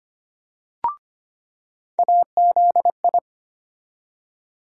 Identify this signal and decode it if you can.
Morse code, heard as a keyed tone, about 25 wpm, 700 Hz. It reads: AZI
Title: USA Call Sign Postfixes - 3 Letters 25wpm